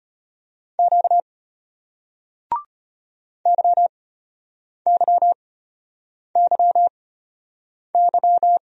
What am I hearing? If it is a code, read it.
Morse code, heard as a keyed tone, about 35 wpm, spaced as for 8 wpm, 700 Hz. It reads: Q YYYY